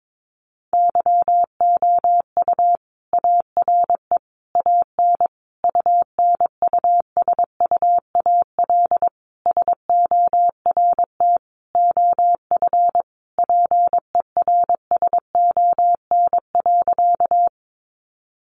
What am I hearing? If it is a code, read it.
Morse code, heard as a keyed tone, about 22 wpm, 700 Hz. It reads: YOU ARE AN UNUSUAL SORT OF PERSON.
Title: Sentences from Top 1400 Words 22wpm